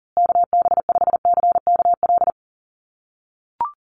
Morse code, heard as a keyed tone, about 40 wpm, 700 Hz. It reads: KB5CKL
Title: USA Call Signs Repeated 3x 40wpm